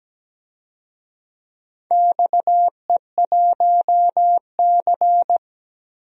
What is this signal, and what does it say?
Morse code, heard as a keyed tone, about 17 wpm, 700 Hz. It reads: XE1C
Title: DXCC-100 Call Signs - 4 Characters - 17wpm